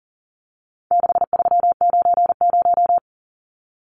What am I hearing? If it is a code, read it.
Morse code, heard as a keyed tone, about 40 wpm, 700 Hz. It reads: 6390
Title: Numbers - 4-Digit 40wpm